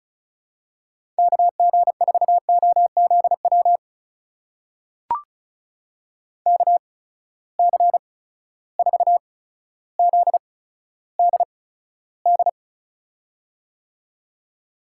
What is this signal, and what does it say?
Morse code, heard as a keyed tone, about 34 wpm, spaced as for 15 wpm, 700 Hz. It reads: KG4OZW KC4ZDD
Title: USA Call Signs - 6 Characters - 8x Character Spacing - 35wpm